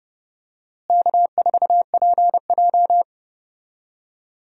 Morse code, heard as a keyed tone, about 32 wpm, 700 Hz. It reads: K4PJ
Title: USA Call Signs - 4 Characters - Repeated 3x - 30wpm